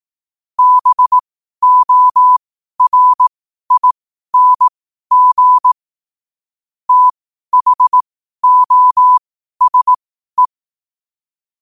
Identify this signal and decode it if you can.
Morse code, heard as a keyed tone, about 18 wpm, spaced as for 12 wpm, 1000 Hz. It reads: BORING THOSE
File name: qotdfiles / 12wpm